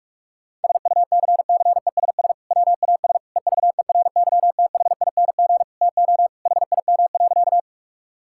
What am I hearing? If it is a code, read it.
Morse code, heard as a keyed tone, about 42 wpm, 700 Hz. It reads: SUCCESS WAS EVERYTHING TO HIM.
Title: Sentences from Top 1000 Words - Review Part 6 - 45wpm